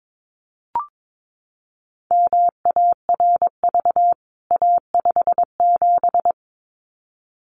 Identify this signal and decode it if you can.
Morse code, heard as a keyed tone, about 22 wpm, 700 Hz. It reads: MARV A57
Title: CWT Exchange 22wpm